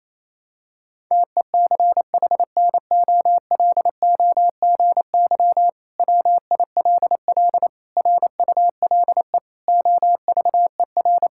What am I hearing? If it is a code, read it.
Morse code, heard as a keyed tone, about 28 wpm, 700 Hz. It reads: TECHNOLOGY WILL RULE OVER
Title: Sentences from Top 1000 Words - Review Part 5 - 28wpm